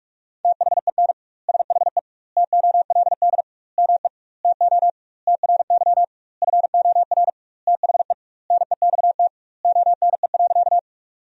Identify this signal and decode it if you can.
Morse code, heard as a keyed tone, about 42 wpm, 700 Hz. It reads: THEN SHE TOLD ME TO TRY FOR THE NEXT ONE.
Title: 2x Word-Spacing - Sentences from Top 400 Words - 45wpm